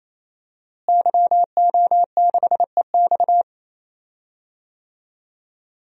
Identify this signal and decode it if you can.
Morse code, heard as a keyed tone, about 28 wpm, 700 Hz. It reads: YO6EX
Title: DXCC-100 Call Signs - 5 Characters - Repeated 3x - 28wpm